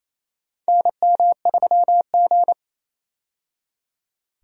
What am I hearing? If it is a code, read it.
Morse code, heard as a keyed tone, about 28 wpm, 700 Hz. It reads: NM3G
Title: USA Call Signs - 4 Characters - 28wpm